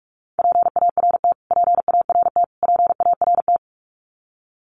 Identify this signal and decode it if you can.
Morse code, heard as a keyed tone, about 44 wpm, 700 Hz. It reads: PART PART PART
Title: Top 500 Words Repeated 45wpm